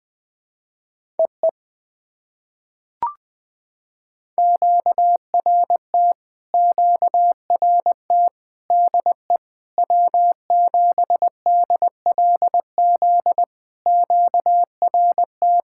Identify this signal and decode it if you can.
Morse code, heard as a keyed tone, about 20 wpm, 700 Hz. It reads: EE QRT QRT DE W7DLZ QRT
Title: POTA QSO Practice - 20wpm